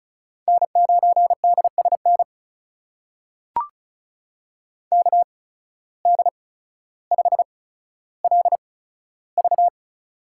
Morse code, heard as a keyed tone, about 36 wpm, spaced as for 15 wpm, 700 Hz. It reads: N9DSN KD5LV
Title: USA Call Signs - 5 Characters - 8x Character Spacing - 35wpm